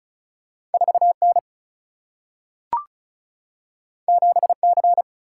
Morse code, heard as a keyed tone, about 36 wpm, 700 Hz. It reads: 4 N 7 C
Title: Single-Letter Single-Number 35wpm